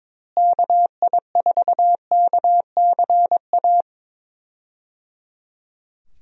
Morse code, heard as a keyed tone, about 22 wpm, 700 Hz. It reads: KI4KCA
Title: USA Call Signs - 6 Characters - Repeated 3x - 22wpm